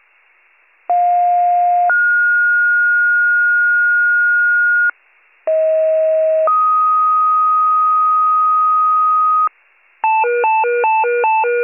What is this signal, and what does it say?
Two-tone page, heard as a one-second tone, then a three-second tone, about 710 then 1430 Hz, about 640 then 1250 Hz, with noise. It is heard from 0.9 to 9.5 s.